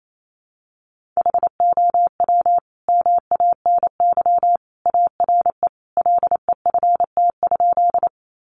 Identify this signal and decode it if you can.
Morse code, heard as a keyed tone, about 28 wpm, 700 Hz. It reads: HOW MANY ARE LEFT?
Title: Sentences from Top 300 Words 28wpm